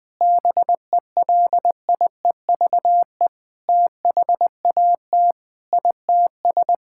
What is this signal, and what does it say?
Morse code, heard as a keyed tone, about 21 wpm, 700 Hz. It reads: BELIEVE THAT ITS
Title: Sentences with 7-Word Phrases - 20wpm